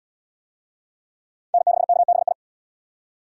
Morse code, heard as a keyed tone, about 68 wpm, 700 Hz. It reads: N6KBI